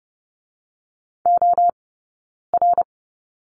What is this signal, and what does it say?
Morse code, heard as a keyed tone, about 30 wpm, 700 Hz. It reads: O R
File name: 092 Lesson 16d - Letter c - Two Characters 30wpm